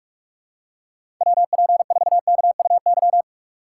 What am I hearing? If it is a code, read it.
Morse code, heard as a keyed tone, about 44 wpm, 690 Hz. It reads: WP4KUY
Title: USA Call Signs - 6x Character Spacing - 45wpm